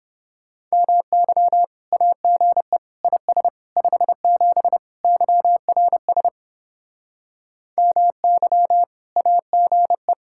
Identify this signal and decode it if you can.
Morse code, heard as a keyed tone, about 31 wpm, 700 Hz. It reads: MY AGE IS 57 YRS MY AGE